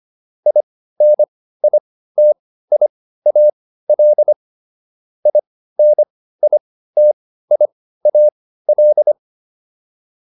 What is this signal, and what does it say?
Morse code, heard as a keyed tone, about 25 wpm, spaced as for 15 wpm, 600 Hz. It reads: INITIAL INITIAL